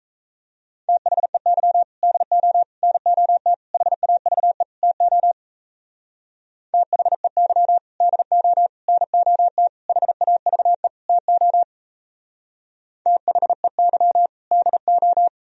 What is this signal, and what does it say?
Morse code, heard as a keyed tone, about 35 wpm, 700 Hz. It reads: THEY DO NOT HAVE TO THEY DO NOT HAVE TO THEY DO
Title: Speed-Racing - Sets of 5 Words 28wpm